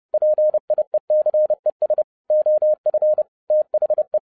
Morse code, heard as a keyed tone, about 30 wpm, 610 Hz. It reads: PIECES OF THE